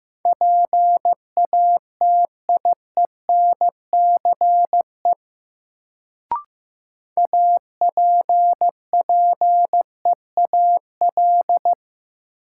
Morse code, heard as a keyed tone, about 15 wpm, 700 Hz. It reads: PATIENCE APPEAL
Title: Sentences from Top 1300 Words - New Word Drill - 15wpm